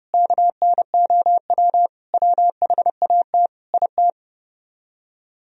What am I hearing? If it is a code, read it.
Morse code, heard as a keyed tone, about 39 wpm, 700 Hz. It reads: KNOW WHAT IT